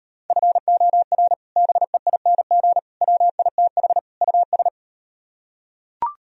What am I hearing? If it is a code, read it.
Morse code, heard as a keyed tone, about 38 wpm, 700 Hz. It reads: FOR BEING WITH US